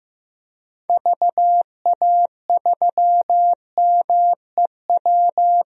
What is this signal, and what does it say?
Morse code, heard as a keyed tone, about 15 wpm, 700 Hz. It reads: VA3MEW
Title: DXCC-100 Call Signs - 6 Characters - Repeated 3x - 15wpm